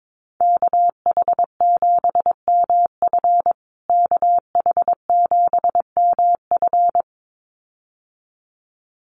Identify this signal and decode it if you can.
Morse code, heard as a keyed tone, about 22 wpm, 700 Hz. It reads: KH7MF KH7MF
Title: USA Call Signs Repeated 2x 22wpm